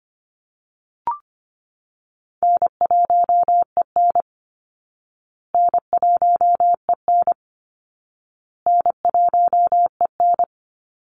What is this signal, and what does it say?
Morse code, heard as a keyed tone, about 25 wpm, 700 Hz. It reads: N1EN N1EN N1EN
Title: Easy Call Signs Repeated 3x - 25wpm